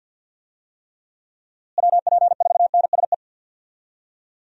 Morse code, heard as a keyed tone, about 52 wpm, 700 Hz. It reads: WP4NSE
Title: USA Call Signs - 6 Characters - Repeated 2x - 50wpm